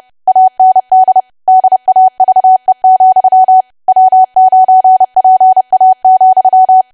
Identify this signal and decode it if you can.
Morse code, heard as a keyed tone, about 30 wpm, 750 Hz. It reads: AND DAVE, W9PA,
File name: cwa / W1AW-Level3 / 30WPM